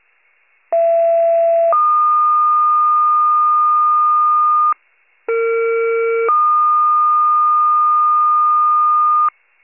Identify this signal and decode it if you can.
Two-tone page, heard as a one-second tone, then a three-second tone, about 670 then 1220 Hz, about 460 then 1220 Hz. It is heard from 0.7 to 9.3 s.